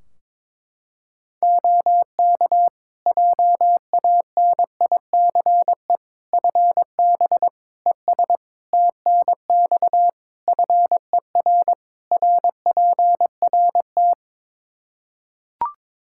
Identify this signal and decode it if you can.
Morse code, heard as a keyed tone, about 22 wpm, 700 Hz. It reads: OK JANICE FB ES TNX FER RPRT
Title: K1USN SST QSO Practice 22wpm